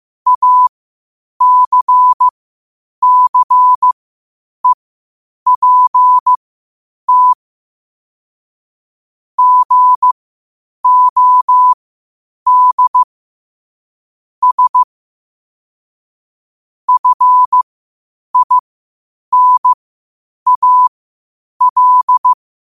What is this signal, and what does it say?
Morse code, heard as a keyed tone, about 15 wpm, spaced as for 8 wpm, 1000 Hz. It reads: ACCEPT GOD S FINAL
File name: qotdfiles / 8wpm